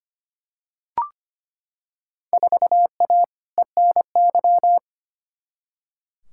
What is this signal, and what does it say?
Morse code, heard as a keyed tone, about 25 wpm, 700 Hz. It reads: 4A ENY
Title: Field Day Exchanges Encore - 25wpm